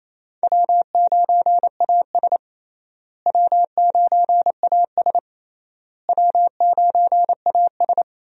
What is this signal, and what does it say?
Morse code, heard as a keyed tone, about 28 wpm, 700 Hz. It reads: W9AS W9AS W9AS
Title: USA Call Signs - 4 Characters - Repeated 3x - 28wpm